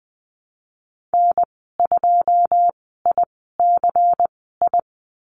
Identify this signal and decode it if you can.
Morse code, heard as a keyed tone, about 20 wpm, spaced as for 14 wpm, 700 Hz. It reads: N2ICI
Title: 128 Lesson 21g - Number 6 - Call Signs 20wpm